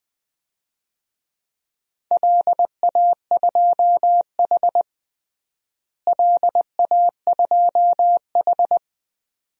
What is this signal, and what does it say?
Morse code, heard as a keyed tone, about 20 wpm, 700 Hz. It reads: LA2H LA2H
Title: DXCC-100 Call Signs - 4 Characters - Repeated 2x - 20wpm